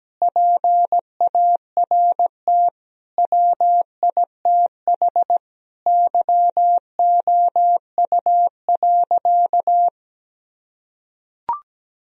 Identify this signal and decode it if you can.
Morse code, heard as a keyed tone, about 17 wpm, 700 Hz. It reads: PART WITH YOU.
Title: Sentences from Top 1000 Words - Review Part 1 - 17wpm